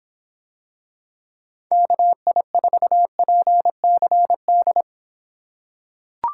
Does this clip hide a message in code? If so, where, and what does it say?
Yes: Morse code from 1.7 to 4.8 s, heard as a keyed tone, about 26 wpm, 700 Hz. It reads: KI4PCD